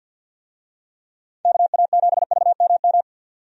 Morse code, heard as a keyed tone, about 50 wpm, 700 Hz. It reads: KA7VMM
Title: USA Call Signs - 6 Characters - 50wpm